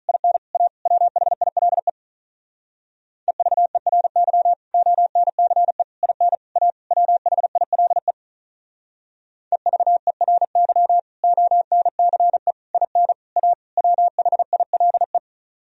Morse code, heard as a keyed tone, about 45 wpm, 700 Hz. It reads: IN A WHILE EVERY ONCE IN A WHILE EVERY ONCE IN A WHILE